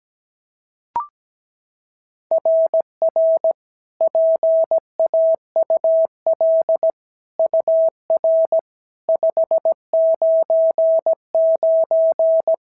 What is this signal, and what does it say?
Morse code, heard as a keyed tone, about 16 wpm, 650 Hz. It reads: RR PAUL UR 599
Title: SOTA QSO Practice 17wpm